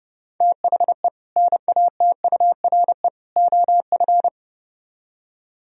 Morse code, heard as a keyed tone, about 29 wpm, 700 Hz. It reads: THE NATURE OF